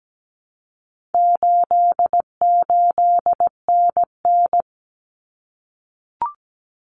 Morse code, heard as a keyed tone, about 17 wpm, 700 Hz. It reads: 88NN
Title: Common Cut Numbers - 4-Digit 17wpm